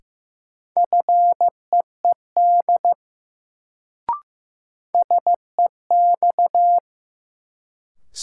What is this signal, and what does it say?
Morse code, heard as a keyed tone, about 15 wpm, 700 Hz. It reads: FEED SEX
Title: Sentences from Top 1600 Words - New Word Drill - 15wpm